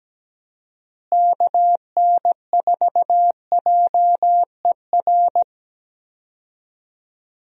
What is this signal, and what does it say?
Morse code, heard as a keyed tone, about 17 wpm, 700 Hz. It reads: KN4JER
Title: USA Call Signs - 6 Characters - Repeated 2x - 17wpm